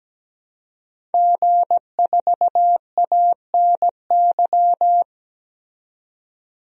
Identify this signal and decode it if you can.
Morse code, heard as a keyed tone, about 17 wpm, 700 Hz. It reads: G4ANY
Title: DXCC-100 Call Signs - 5 Characters - 17wpm